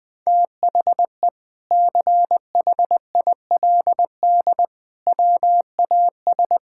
Morse code, heard as a keyed tone, about 20 wpm, 700 Hz. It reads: THE CHILD WAS
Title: Sentences from Top 1000 Words - Review Part 6 - 20wpm